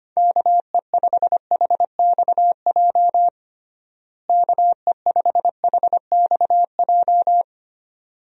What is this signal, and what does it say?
Morse code, heard as a keyed tone, about 25 wpm, 700 Hz. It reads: KE5HXJ KE5HXJ
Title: USA Call Signs - 6 Characters - Repeated 2x - 25wpm